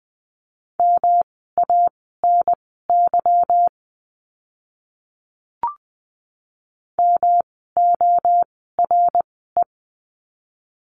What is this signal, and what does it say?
Morse code, heard as a keyed tone, about 20 wpm, spaced as for 15 wpm, 700 Hz. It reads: MANY MORE
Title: 192 Lesson 31f - Letter v - Words 20wpm